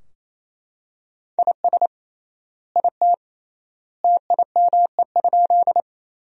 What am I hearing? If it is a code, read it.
Morse code, heard as a keyed tone, about 28 wpm, 700 Hz. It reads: IS IT TIME?
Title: 3x Word-Spacing - Sentences from Top 100 Words - 28wpm